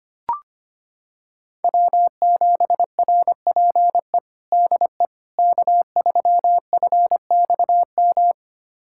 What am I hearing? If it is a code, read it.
Morse code, heard as a keyed tone, about 25 wpm, 700 Hz. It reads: W7RPE DE K3FXM
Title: USA Callsign DE Callsign - 5 Characters - 25wpm